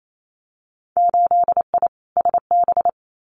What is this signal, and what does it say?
Morse code, heard as a keyed tone, about 28 wpm, 700 Hz. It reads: 8I SB
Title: Winter Field Day Exchanges Encore 28wpm